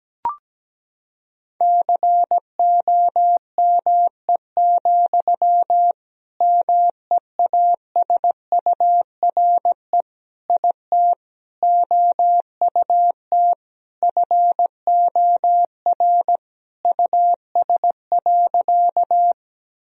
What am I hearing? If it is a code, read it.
Morse code, heard as a keyed tone, about 17 wpm, 700 Hz. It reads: COME, MEASURE IT OUT FOR US.
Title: Sentences from Top 1000 Words - Review Part 2 - 17wpm